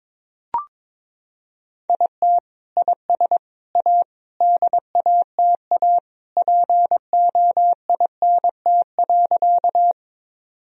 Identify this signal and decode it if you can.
Morse code, heard as a keyed tone, about 22 wpm, 700 Hz. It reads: IT IS A DATA POINT.
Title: Sentences from Top 1000 Words - Review Part 3 - 22wpm